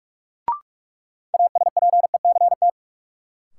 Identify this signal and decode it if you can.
Morse code, heard as a keyed tone, about 46 wpm, 700 Hz. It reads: ASPECT
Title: Sentences from Top 1300 Words - New Word Drill Alt - 45wpm